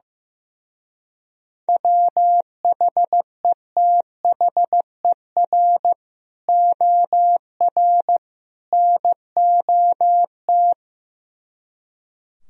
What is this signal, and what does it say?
Morse code, heard as a keyed tone, about 15 wpm, 700 Hz. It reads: WHETHER OR NOT
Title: Speed-Racing - Sets of 3 Words 15wpm